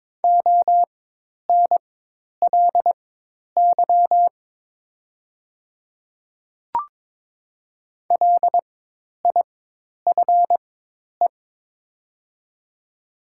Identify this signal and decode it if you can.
Morse code, heard as a keyed tone, about 22 wpm, spaced as for 10 wpm, 700 Hz. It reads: ONLY LIFE
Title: 4x Character Spacing - Top 500 Words - 22wpm